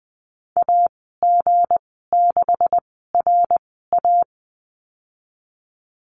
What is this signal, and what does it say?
Morse code, heard as a keyed tone, about 20 wpm, spaced as for 15 wpm, 700 Hz. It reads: AG6RA
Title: 167 Lesson 27g - Number 7 - Call Signs 20wpm